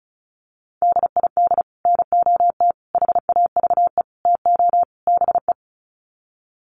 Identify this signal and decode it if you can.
Morse code, heard as a keyed tone, about 34 wpm, 700 Hz. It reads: DID NOT HAVE TO BE
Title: Sets of 5 Words Encore2 35wpm